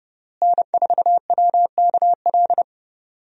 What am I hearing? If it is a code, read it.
Morse code, heard as a keyed tone, about 30 wpm, 700 Hz. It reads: N4WKL
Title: USA Call Signs - 5 Characters - Repeated 3x - 30wpm